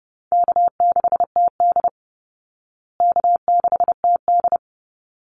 Morse code, heard as a keyed tone, about 30 wpm, 700 Hz. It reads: K6TD K6TD
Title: Easy Call Signs Repeated 2x - 30wpm